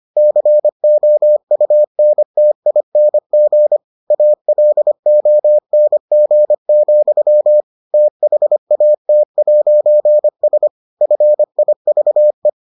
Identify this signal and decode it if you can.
Morse code, heard as a keyed tone, about 25 wpm, 600 Hz. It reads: COUNTING ALONG, THAT'S FIVE